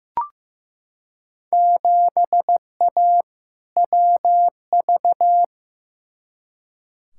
Morse code, heard as a keyed tone, about 15 wpm, 700 Hz. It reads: 7A WV
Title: Field Day Exchanges Encore - 15wpm